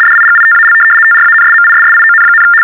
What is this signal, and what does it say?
RTTY, about 75 baud. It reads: DIGITAL TRANSMISSION,
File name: teletype-RTTY-75